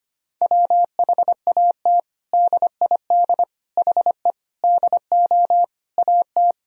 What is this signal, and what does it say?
Morse code, heard as a keyed tone, about 25 wpm, 700 Hz. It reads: WHAT DID HE DO AT
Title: Sentences from Top 1000 Words - Review Part 1 - 25wpm